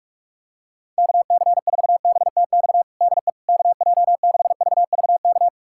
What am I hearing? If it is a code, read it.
Morse code, heard as a keyed tone, about 46 wpm, 700 Hz. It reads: KC4BTX DE KJ6VVK
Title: USA Callsign DE Callsign - 6 Characters - 45wpm